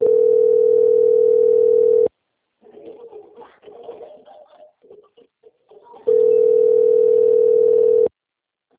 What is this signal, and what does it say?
Ringback tone (North America), 440 + 480 Hz.
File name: out- - - - - .24483